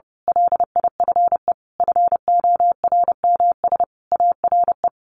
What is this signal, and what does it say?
Morse code, heard as a keyed tone, about 30 wpm, 700 Hz. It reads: LIFE FORMS ARE
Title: A Sound Of Thunder 30wpm